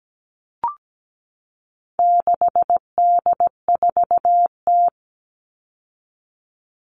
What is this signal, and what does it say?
Morse code, heard as a keyed tone, about 17 wpm, 700 Hz. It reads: 6D4T
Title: All Cut Numbers - 4-Digit 17wpm